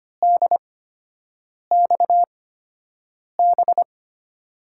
Morse code, heard as a keyed tone, about 25 wpm, spaced as for 7 wpm, 700 Hz. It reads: DXB
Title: USA Call Signs - 6 Characters - 8x Character Spacing - 25wpm